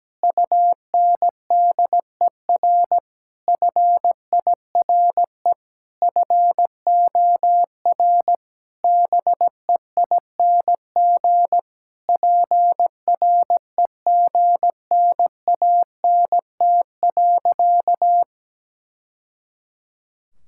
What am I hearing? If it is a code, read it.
Morse code, heard as a keyed tone, about 17 wpm, 700 Hz. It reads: UNDER FIRE FOR BEING PREGNANT.
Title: Sentences from Top 1000 Words - Review Part 6 - 17wpm